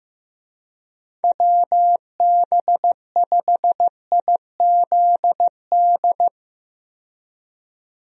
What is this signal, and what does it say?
Morse code, heard as a keyed tone, about 15 wpm, 700 Hz. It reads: WB5IZD
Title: USA Call Signs - 6 Characters - 15wpm